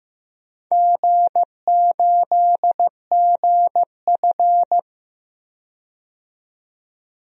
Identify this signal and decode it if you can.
Morse code, heard as a keyed tone, about 15 wpm, 700 Hz. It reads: G8GF
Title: DXCC-100 Call Signs - 4 Characters - 15wpm